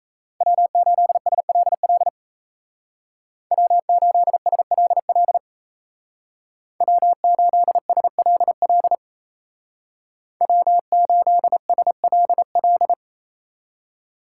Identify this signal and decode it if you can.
Morse code, heard as a keyed tone, about 34 wpm, 700 Hz. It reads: W8SLL W8SLL W8SLL W8SLL
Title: Speed Racing - USA 1x3 Call Signs 28wpm